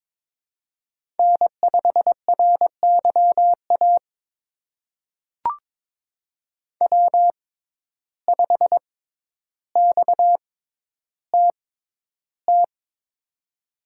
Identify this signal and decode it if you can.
Morse code, heard as a keyed tone, about 22 wpm, spaced as for 11 wpm, 700 Hz. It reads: N5RYA W5XTT
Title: USA Call Signs - 6x Character Spacing - 22wpm